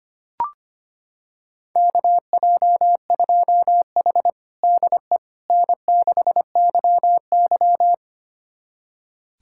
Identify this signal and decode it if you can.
Morse code, heard as a keyed tone, about 25 wpm, 700 Hz. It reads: KJ2H DE N6YY
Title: USA Callsign DE Callsign - 4 Characters - 25wpm